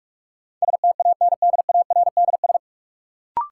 Morse code, heard as a keyed tone, about 41 wpm, 700 Hz. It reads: STANDARDS
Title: Sentences from Top 1000 Words - Review Part 6 - New Word Drill - 45wpm